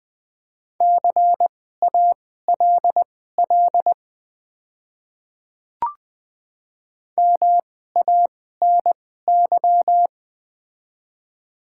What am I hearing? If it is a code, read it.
Morse code, heard as a keyed tone, about 20 wpm, spaced as for 14 wpm, 700 Hz. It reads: CALL MANY
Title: 2x Character Spacing - Top 100 Words - 20wpm